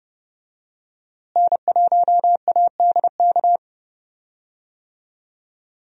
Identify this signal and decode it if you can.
Morse code, heard as a keyed tone, about 31 wpm, 700 Hz. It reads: N1ADK
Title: USA Call Signs - 5 Characters - Repeated 3x - 30wpm